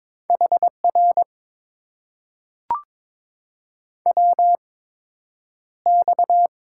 Morse code, heard as a keyed tone, about 22 wpm, spaced as for 9 wpm, 700 Hz. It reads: HR WX
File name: Top 90 QSO Elements - 8x Character Spacing - 22wpm